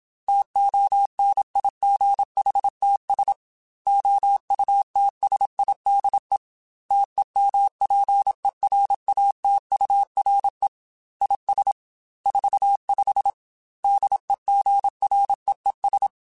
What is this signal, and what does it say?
Morse code, heard as a keyed tone, about 26 wpm, 800 Hz. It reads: TONIGHTS OUTSIDE TEMPERATURE IS 45 DEGREES
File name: Weather Sets / Weather Sets-mp3s / Weather 3 17 @25